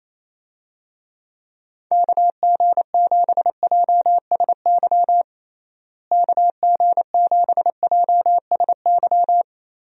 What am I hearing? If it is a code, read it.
Morse code, heard as a keyed tone, about 28 wpm, 700 Hz. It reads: KG7JSY KG7JSY
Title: USA Call Signs - 6 Characters - Repeated 2x - 28wpm